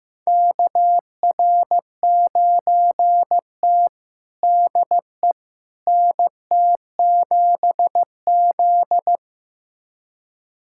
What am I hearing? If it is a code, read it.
Morse code, heard as a keyed tone, about 15 wpm, 700 Hz. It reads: KR9T DE NT7Z
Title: USA Callsign DE Callsign - 4 Characters - 15wpm